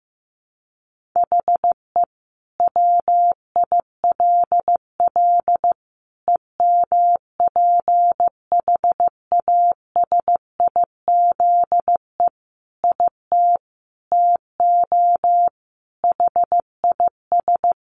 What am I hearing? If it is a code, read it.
Morse code, heard as a keyed tone, about 15 wpm, 700 Hz. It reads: HE WILL EMPHASIZE IT TO HIS
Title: Sentences from Top 1500 Words 15wpm